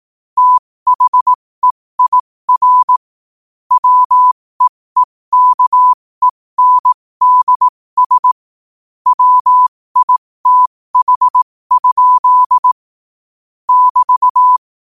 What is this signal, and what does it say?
Morse code, heard as a keyed tone, about 18 wpm, spaced as for 15 wpm, 1000 Hz. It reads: THEIR WEEKENDS WITH? <BT>
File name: qotdfiles / 15wpm